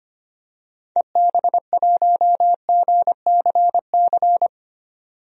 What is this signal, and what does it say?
Morse code, heard as a keyed tone, about 25 wpm, 700 Hz. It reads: EB1GCC
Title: DXCC-100 Call Signs - 6 Characters - 25wpm